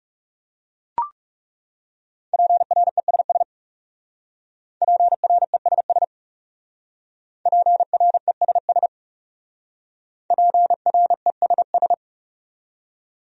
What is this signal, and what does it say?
Morse code, heard as a keyed tone, about 37 wpm, 700 Hz. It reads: PRESS PRESS PRESS PRESS
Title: Speed-Racing Top 500 Words 30wpm